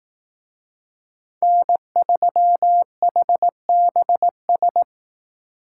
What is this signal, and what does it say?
Morse code, heard as a keyed tone, about 18 wpm, 700 Hz. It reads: N3HBS
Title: Speed Racing - USA 1x3 Call Signs 15wpm